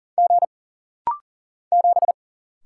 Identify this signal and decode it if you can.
Morse code, heard as a keyed tone, about 40 wpm, 700 Hz. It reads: G 7
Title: Single Letter-Number - Rapid-Fire 40wpm